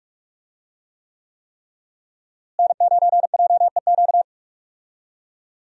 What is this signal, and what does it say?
Morse code, heard as a keyed tone, about 46 wpm, 700 Hz. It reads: N9JEQ
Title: USA Call Signs - 5 Characters - Repeated 3x - 45wpm